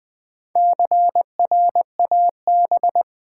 Morse code, heard as a keyed tone, about 20 wpm, 700 Hz. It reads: CRAB
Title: Easy Words with Hints - 20wpm